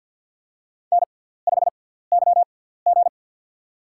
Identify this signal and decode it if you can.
Morse code, heard as a keyed tone, about 53 wpm, spaced as for 17 wpm, 700 Hz. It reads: N5YG
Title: USA Call Signs - 6x Character Spacing - 50wpm